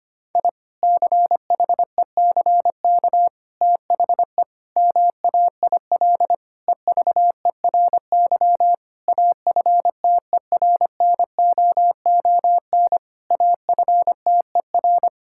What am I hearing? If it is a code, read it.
Morse code, heard as a keyed tone, about 25 wpm, 700 Hz. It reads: I CHECK THE MAIL EVERY AFTERNOON AFTER
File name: practice / 25wpm / Difficult Sentences with Hints - 25wpm